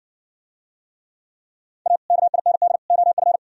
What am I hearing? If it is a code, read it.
Morse code, heard as a keyed tone, about 55 wpm, 700 Hz. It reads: A KIND OF